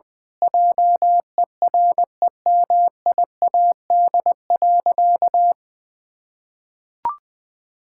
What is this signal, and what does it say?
Morse code, heard as a keyed tone, about 21 wpm, 700 Hz. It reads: JEREMIAD.
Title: Pangram Challenge v2 - 20wpm